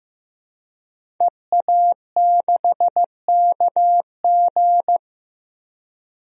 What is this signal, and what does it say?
Morse code, heard as a keyed tone, about 15 wpm, 700 Hz. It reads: EA6KG
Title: DXCC-100 Call Signs - 5 Characters - Repeated 2x - 15wpm